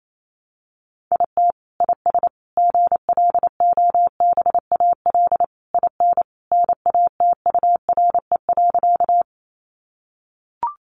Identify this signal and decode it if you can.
Morse code, heard as a keyed tone, about 28 wpm, 700 Hz. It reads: IT IS GLOBAL IN NATURE.
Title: Sentences from Top 1200 Words 28wpm